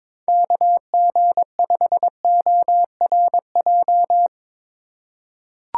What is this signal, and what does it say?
Morse code, heard as a keyed tone, about 22 wpm, 700 Hz. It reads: KG5ORJ
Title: USA Call Signs - 6 Characters - 2x Character Spacing - 22wpm